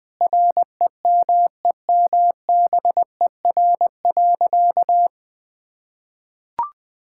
Morse code, heard as a keyed tone, about 20 wpm, 700 Hz. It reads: REMEMBER.
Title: Sentences with 8-Word Phrases - Spoken Beginning - 20wpm